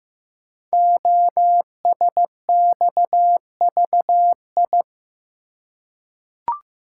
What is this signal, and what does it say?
Morse code, heard as a keyed tone, about 15 wpm, 700 Hz. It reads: OSXVI
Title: Code Groups - 5 Letters - 15wpm